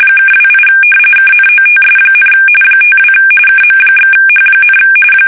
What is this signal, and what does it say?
RTTY, about 45.45 baud. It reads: (28 KNOTS, 52 KM/H,
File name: fsk850 / battleships